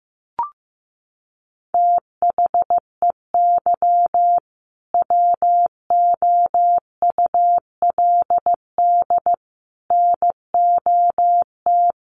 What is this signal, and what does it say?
Morse code, heard as a keyed tone, about 15 wpm, 700 Hz. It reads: THEY WOULD NOT
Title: Sentences from Top 1400 Words 15wpm